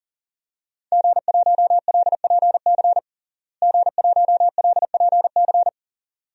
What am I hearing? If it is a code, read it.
Morse code, heard as a keyed tone, about 40 wpm, 700 Hz. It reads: G1LPC G1LPC
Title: DXCC-100 Call Signs - 5 Characters - Repeated 2x - 40wpm